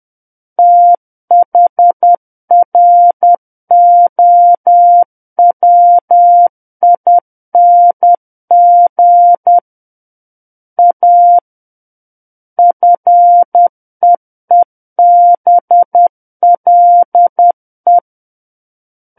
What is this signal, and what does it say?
Morse code, heard as a keyed tone, about 10 wpm, 700 Hz. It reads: THROWING A FEEBLE
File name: Book One / 01-Chapter 1 - 10 WPM